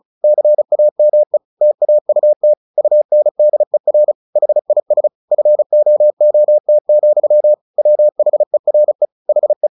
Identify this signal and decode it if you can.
Morse code, heard as a keyed tone, about 34 wpm, 600 Hz. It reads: CAME TAUT UNDER HIS FOOT, WHERE HE
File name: the old man and the sea / the old man and the sea 1 35wpm